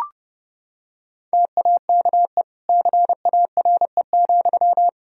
Morse code, heard as a keyed tone, about 30 wpm, 700 Hz. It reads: TAKE CARE,
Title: Sentences from Top 1000 Words - Review Part 4 - 30wpm